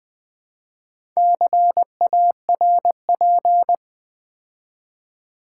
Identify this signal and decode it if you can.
Morse code, heard as a keyed tone, about 20 wpm, 700 Hz. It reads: CARP
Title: Build A Word - 2 Letters At A Time - 6-Letter Words - 20wpm